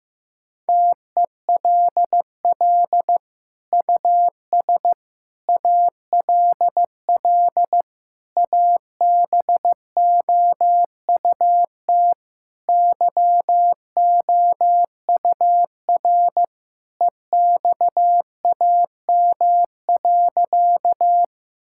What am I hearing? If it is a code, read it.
Morse code, heard as a keyed tone, about 15 wpm, 700 Hz. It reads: TELL US ALL ABOUT YOUR EXAM.
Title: Sentences from Top 1000 Words - Review Part 5 - 15wpm